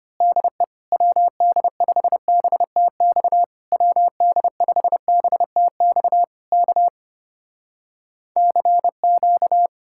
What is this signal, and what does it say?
Morse code, heard as a keyed tone, about 24 wpm, 700 Hz. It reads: DE WD5BTX WD5BTX K CQ
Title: Speed-Racing Common QSO Phrases - 25wpm